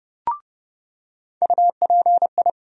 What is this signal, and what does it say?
Morse code, heard as a keyed tone, about 30 wpm, 700 Hz. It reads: UPI
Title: USA Call Sign Postfixes - 3 Letters 30wpm